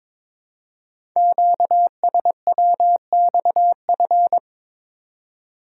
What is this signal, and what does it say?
Morse code, heard as a keyed tone, about 22 wpm, 700 Hz. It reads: QSWXF
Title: Code Groups - 5 Letters - 22wpm